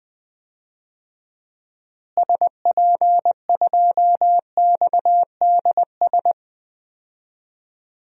Morse code, heard as a keyed tone, about 21 wpm, 700 Hz. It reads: SP2XDS
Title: DXCC-100 Call Signs - 6 Characters - Repeated 2x - 20wpm